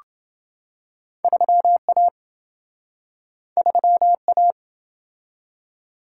Morse code, heard as a keyed tone, about 27 wpm, 700 Hz. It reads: 3A 3A